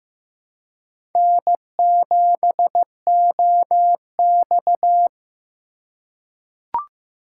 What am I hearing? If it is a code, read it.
Morse code, heard as a keyed tone, about 15 wpm, 700 Hz. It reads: N7OX
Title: USA Call Signs - 4 Characters - 2x Character Spacing - 15wpm